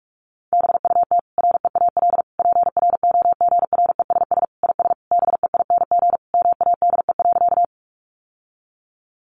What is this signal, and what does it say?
Morse code, heard as a keyed tone, about 44 wpm, 700 Hz. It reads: BUT REAL PROGRESS IS BEING MADE.
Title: Sentences from Top 1400 Words 45wpm